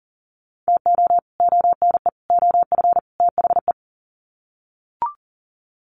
Morse code, heard as a keyed tone, about 40 wpm, 700 Hz. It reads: TO ONE OF THE
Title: Sets of 4 Words 40wpm